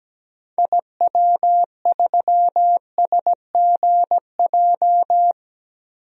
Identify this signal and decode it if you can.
Morse code, heard as a keyed tone, about 17 wpm, 700 Hz. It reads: IW3SGJ
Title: DXCC-100 Call Signs - 6 Characters - 17wpm